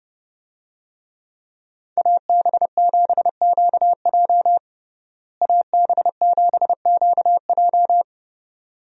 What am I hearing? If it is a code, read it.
Morse code, heard as a keyed tone, about 30 wpm, 700 Hz. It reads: AB7QJ AB7QJ
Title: USA Call Signs - 5 Characters - Repeated 2x - 30wpm